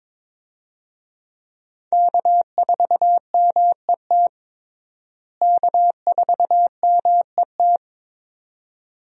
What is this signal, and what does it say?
Morse code, heard as a keyed tone, about 22 wpm, 700 Hz. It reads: K4MET K4MET
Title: USA Call Signs - 5 Characters - Repeated 2x - 22wpm